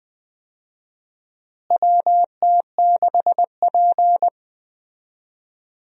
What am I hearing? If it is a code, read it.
Morse code, heard as a keyed tone, about 20 wpm, 700 Hz. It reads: WT6P
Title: USA Call Signs - 4 Characters - Repeated 2x - 20wpm